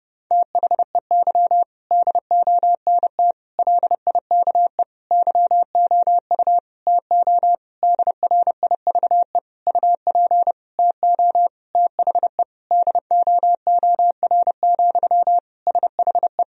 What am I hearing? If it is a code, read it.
Morse code, heard as a keyed tone, about 32 wpm, 700 Hz. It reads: THEY DONT LIKE YOU TO DRIVE UP TO THE DOOR, SHE
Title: Sentences from Top 1000 Words - Review Part 2 - 30wpm